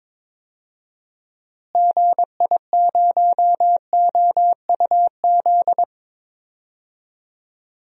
Morse code, heard as a keyed tone, about 22 wpm, 700 Hz. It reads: GI0OUZ
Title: DXCC-100 Call Signs - 6 Characters - Repeated 3x - 22wpm